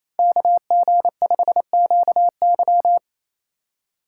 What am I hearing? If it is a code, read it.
Morse code, heard as a keyed tone, about 28 wpm, 700 Hz. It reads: KG5QY
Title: USA Call Signs - 5 Characters - 4x Character Spacing - 28wpm